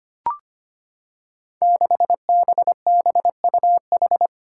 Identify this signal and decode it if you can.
Morse code, heard as a keyed tone, about 25 wpm, 700 Hz. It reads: 6BBUH
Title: Five Challenging Letter-Number Combinations 25wpm